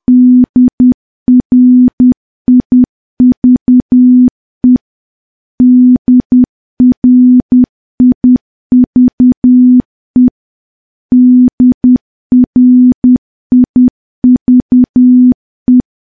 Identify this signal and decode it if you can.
Morse code, heard as a keyed tone, about 10 wpm, 250 Hz. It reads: DRIVE DRIVE DRIVE
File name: morse / morse-training-10WPM